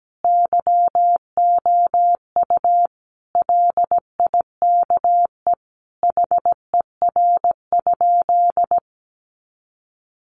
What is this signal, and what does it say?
Morse code, heard as a keyed tone, about 17 wpm, 700 Hz. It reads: YOU LIKE HER?
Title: Sentences from Top 100 Words 17wpm